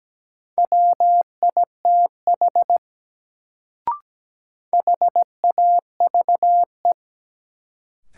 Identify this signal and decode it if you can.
Morse code, heard as a keyed tone, about 17 wpm, 700 Hz. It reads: WITH HAVE
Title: Sentences from Top 100 Words - New Word Drill - 17wpm